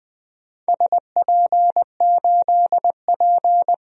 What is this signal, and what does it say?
Morse code, heard as a keyed tone, about 20 wpm, 700 Hz. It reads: SP8P
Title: DXCC-100 Call Signs - 4 Characters - 20wpm